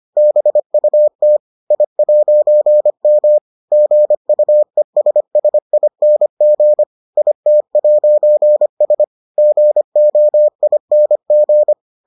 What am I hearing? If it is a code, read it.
Morse code, heard as a keyed tone, about 25 wpm, 600 Hz. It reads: BUT I'M GUESSING IT'S GOING